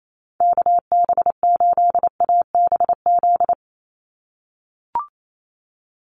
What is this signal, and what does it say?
Morse code, heard as a keyed tone, about 28 wpm, 700 Hz. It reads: KB8ABZ
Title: USA Call Signs 28wpm